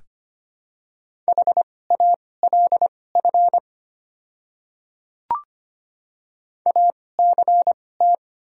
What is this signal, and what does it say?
Morse code, heard as a keyed tone, about 25 wpm, spaced as for 18 wpm, 690 Hz. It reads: HALF ACT
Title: 2x Character Spacing - Top 500 Words - 25wpm